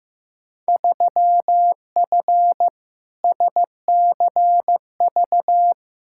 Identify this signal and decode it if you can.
Morse code, heard as a keyed tone, about 15 wpm, 700 Hz. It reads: 3F SCV
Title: Field Day Exchanges - 15wpm